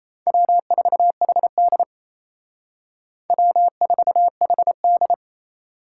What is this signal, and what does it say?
Morse code, heard as a keyed tone, about 35 wpm, 700 Hz. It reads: W4HD W4HD
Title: Speed Racing - USA 1x2 Call Signs 28wpm